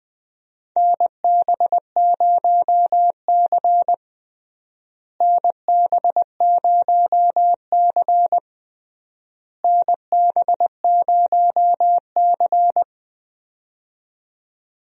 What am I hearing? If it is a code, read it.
Morse code, heard as a keyed tone, about 21 wpm, 690 Hz. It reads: NB0C NB0C NB0C
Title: USA Call Signs - 4 Characters - Repeated 3x - 20wpm